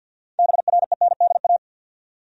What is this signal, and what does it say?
Morse code, heard as a keyed tone, about 52 wpm, 700 Hz. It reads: BRENDA
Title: USA Common Names - 50wpm